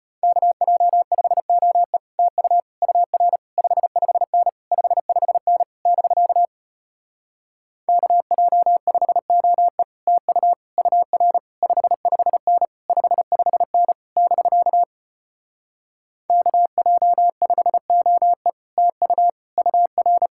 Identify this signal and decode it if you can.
Morse code, heard as a keyed tone, about 33 wpm, 700 Hz. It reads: KJ5OE TU UR 55N 55N <BK> KJ5OE TU UR 55N 55N <BK> KJ5OE TU UR